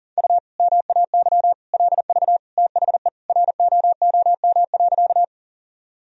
Morse code, heard as a keyed tone, about 40 wpm, 700 Hz. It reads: U MAY LV THE ROOM.
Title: Sentences from Top 400 Words and Abbreviations 40wpm